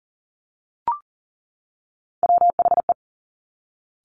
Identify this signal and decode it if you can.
Morse code, heard as a keyed tone, about 40 wpm, 700 Hz. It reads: WHE
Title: Top Three Letter Combinations in Words 40wpm